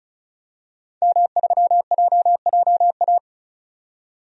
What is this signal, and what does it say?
Morse code, heard as a keyed tone, about 36 wpm, 700 Hz. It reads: M3JJA
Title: DXCC-100 Call Signs - 5 Characters - 35wpm